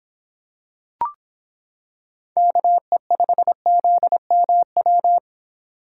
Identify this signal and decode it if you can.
Morse code, heard as a keyed tone, about 26 wpm, 700 Hz. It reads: KE5ZMW